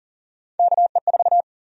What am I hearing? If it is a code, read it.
Morse code, heard as a keyed tone, about 40 wpm, 700 Hz. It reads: KE4
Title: USA Call Sign Prefixes - 40wpm